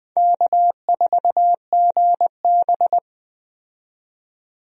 Morse code, heard as a keyed tone, about 20 wpm, 700 Hz. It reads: K4GB
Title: USA Call Signs - 4 Characters - 20wpm